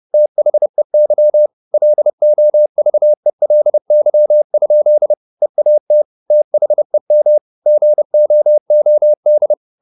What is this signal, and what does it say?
Morse code, heard as a keyed tone, about 30 wpm, 600 Hz. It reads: THEY LOVELY? EAT THEM GOOD
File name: the old man and the sea / the old man and the sea 1 30wpm